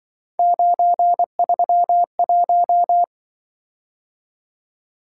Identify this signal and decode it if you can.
Morse code, heard as a keyed tone, about 24 wpm, 700 Hz. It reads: 931
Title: Speed-Racing - Three Letter-Number Combinations - 20wpm